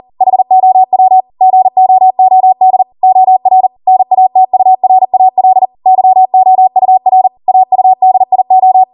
Morse code, heard as a keyed tone, about 40 wpm, 750 Hz. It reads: HOW GOOD OR NATURAL YOUR AUDIO